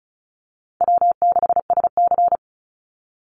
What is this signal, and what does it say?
Morse code, heard as a keyed tone, about 34 wpm, 700 Hz. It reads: W6SC
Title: Easy Call Signs Repeated 2x - 35wpm